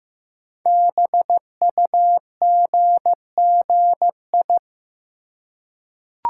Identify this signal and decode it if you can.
Morse code, heard as a keyed tone, about 15 wpm, 700 Hz. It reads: BUGGI
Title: Code Groups - 5 Letter:Numbers - 15wpm